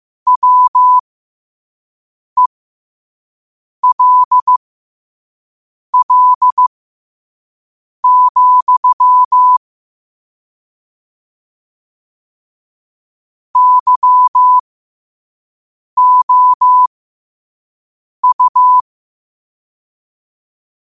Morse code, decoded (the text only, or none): WELL, YOU